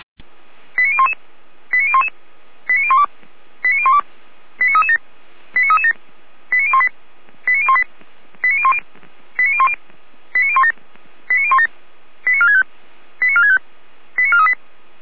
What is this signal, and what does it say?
Five-tone selective call (ZVEI), heard as quick runs of five tones, tones of about 70 ms. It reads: 89011 89011 89012 89012 89337 89337 89018 89018 89010 89010 89017 89017 89465 89465 89438